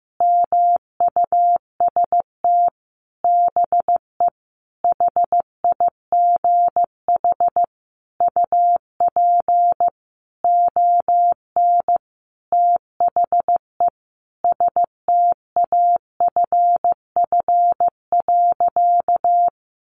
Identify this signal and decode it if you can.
Morse code, heard as a keyed tone, about 15 wpm, 700 Hz. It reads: MUST BE HIGH UP ON THE STAFF.
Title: Sentences from Top 1200 Words 15wpm